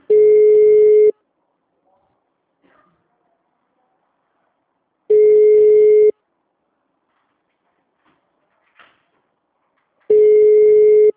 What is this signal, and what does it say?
Ringback tone (Europe, 425 Hz).
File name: out- -2009- - - .375